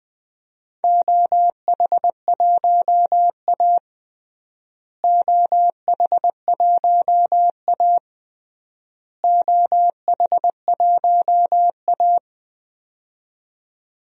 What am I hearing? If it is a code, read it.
Morse code, heard as a keyed tone, about 21 wpm, 700 Hz. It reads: OH1A OH1A OH1A
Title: DXCC-100 Call Signs - 4 Characters - Repeated 3x - 20wpm